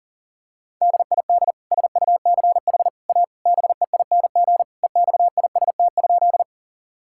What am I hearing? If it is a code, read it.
Morse code, heard as a keyed tone, about 40 wpm, 700 Hz. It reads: DID SUCH A BEING EXIST?
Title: Sentences from Top 1000 Words - Review Part 5 - 40wpm